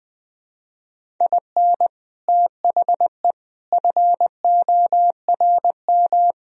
Morse code, heard as a keyed tone, about 24 wpm, 700 Hz. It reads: IN THE FORM
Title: Speed-Racing - Sets of 5 Words 20wpm